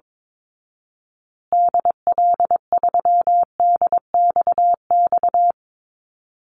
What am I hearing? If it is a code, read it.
Morse code, heard as a keyed tone, about 22 wpm, 700 Hz. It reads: DL3DXX
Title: Call Signs Repeated 3x - 22wpm